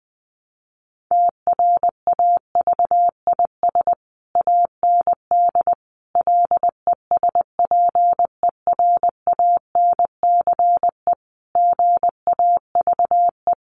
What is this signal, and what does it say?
Morse code, heard as a keyed tone, about 20 wpm, 700 Hz. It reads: TRAVIS AND LESPERANCE GAVE
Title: A Sound Of Thunder 20wpm